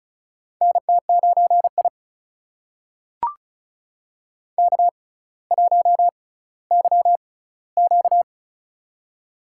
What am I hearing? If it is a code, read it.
Morse code, heard as a keyed tone, about 36 wpm, spaced as for 18 wpm, 700 Hz. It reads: NT9I K1YQ
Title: USA Call Signs - 4 Characters - 6x Character Spacing - 35wpm